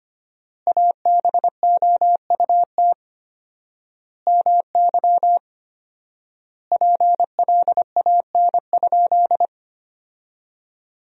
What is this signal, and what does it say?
Morse code, heard as a keyed tone, about 25 wpm, 700 Hz. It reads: ABOUT MY PLAN?
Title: 4x Word-Spacing - Sentences from Top 500 Words - 25wpm